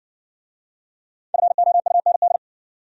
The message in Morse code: FOUND

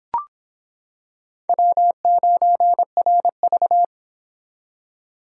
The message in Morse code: W9RV

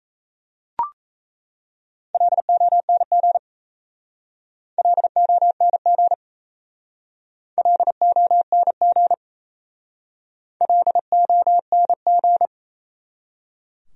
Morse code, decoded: LONG LONG LONG LONG

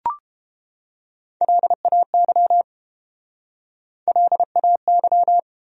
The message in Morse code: LAY LAY

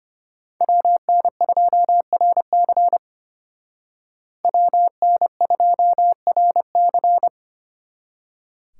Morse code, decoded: WN2RC WN2RC

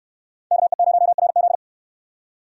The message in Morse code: KE0RZ

